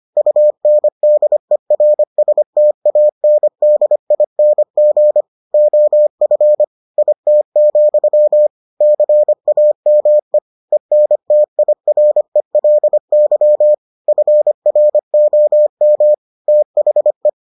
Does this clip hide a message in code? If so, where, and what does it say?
Yes: Morse code from 0.2 to 17.3 s, heard as a keyed tone, about 25 wpm, 600 Hz. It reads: UNDERSTANDING OF IT, CAME ENTIRELY FROM THE